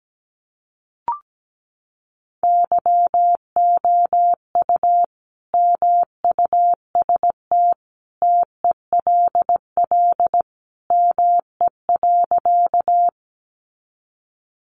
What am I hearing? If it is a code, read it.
Morse code, heard as a keyed tone, about 17 wpm, 700 Hz. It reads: YOU MUST TELL ME.
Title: Sentences from Top 200 Words 17wpm